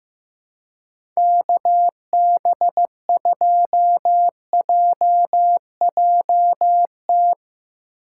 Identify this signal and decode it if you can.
Morse code, heard as a keyed tone, about 15 wpm, 700 Hz. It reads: KB2JJT
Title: USA Call Signs - 6 Characters - 2x Character Spacing - 15wpm